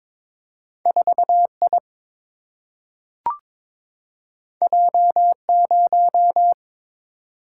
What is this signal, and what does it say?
Morse code, heard as a keyed tone, about 22 wpm, 700 Hz. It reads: 4I J0
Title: Single-Letter Single-Number 22wpm